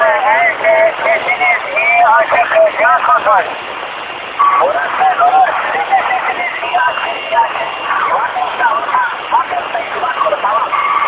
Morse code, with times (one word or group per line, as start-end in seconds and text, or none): none